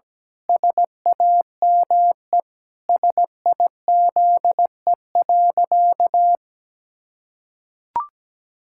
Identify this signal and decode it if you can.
Morse code, heard as a keyed tone, about 17 wpm, 700 Hz. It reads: SAME SIZE.
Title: Pangram Challenge v2 - 17wpm